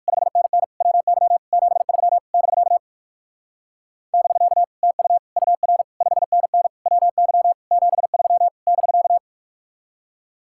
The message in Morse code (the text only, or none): 5NN WY 73 <BK> <BK> TU UR 5NN WY 73 <BK>